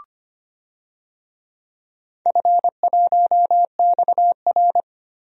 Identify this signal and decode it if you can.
Morse code, heard as a keyed tone, about 25 wpm, 700 Hz. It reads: F1XR
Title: DXCC-100 Call Signs - 4 Characters - 25wpm